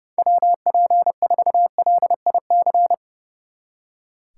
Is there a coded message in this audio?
Yes: Morse code from 0.2 to 2.9 s, heard as a keyed tone, about 40 wpm, 700 Hz. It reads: WP4LIC